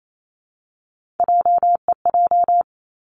Morse code, heard as a keyed tone, about 28 wpm, 700 Hz. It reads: JEJ
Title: Three Challenging Letter-Number Combinations 28wpm